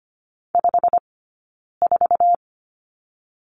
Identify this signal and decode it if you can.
Morse code, heard as a keyed tone, about 25 wpm, spaced as for 9 wpm, 690 Hz. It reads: 54
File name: course / 25:10wpm / 092 Lesson 16d - Letter c - Two Characters 25wpm